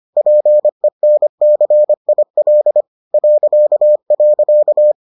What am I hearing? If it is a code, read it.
Morse code, heard as a keyed tone, about 25 wpm, 600 Hz. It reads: PENCIL ..